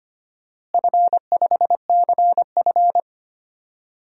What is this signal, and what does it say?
Morse code, heard as a keyed tone, about 25 wpm, 700 Hz. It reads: F5CF
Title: DXCC-100 Call Signs - 4 Characters - Repeated 3x - 25wpm